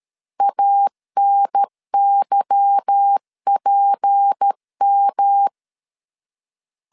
Morse code, heard as a keyed tone, about 13 wpm, 800 Hz. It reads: ANYPM
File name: LEZIONI / 8 YJ45 / 8 Y 12wpm 2228